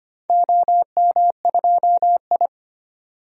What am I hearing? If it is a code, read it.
Morse code, heard as a keyed tone, about 25 wpm, 700 Hz. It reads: OM2I
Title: DXCC-100 Call Signs - 4 Characters - 25wpm